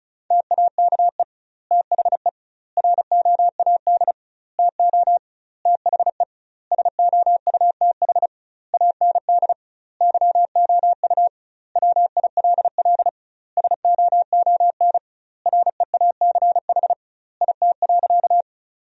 Morse code, heard as a keyed tone, about 36 wpm, 700 Hz. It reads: TAKE THE ROAD TO THE SOUTH AND YOU WILL SOON REACH IT.
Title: 2x Word-Spacing - Sentences from Top 400 Words - 35wpm